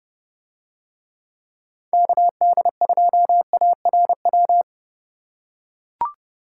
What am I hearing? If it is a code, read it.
Morse code, heard as a keyed tone, about 28 wpm, 700 Hz. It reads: KD2ARW